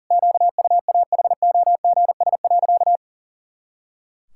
Morse code, heard as a keyed tone, about 40 wpm, 700 Hz. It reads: QUAHOGS.
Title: Pangram Challenge v2 - 40wpm